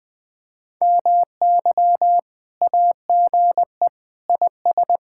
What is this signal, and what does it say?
Morse code, heard as a keyed tone, about 20 wpm, 700 Hz. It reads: MY AGE IS
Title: Speed-Racing Common QSO Phrases - 20wpm